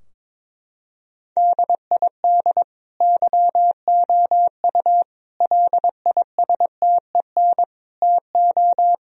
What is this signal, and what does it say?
Morse code, heard as a keyed tone, about 22 wpm, 700 Hz. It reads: DID YOU LISTEN TO
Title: Sentences from Top 1000 Words - Review Part 4 - 22wpm